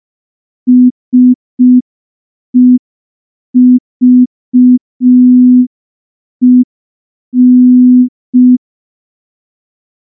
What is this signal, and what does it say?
Morse code, heard as a keyed tone, about 5 wpm, 250 Hz. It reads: SEVEN
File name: morse / morse-training-05WPM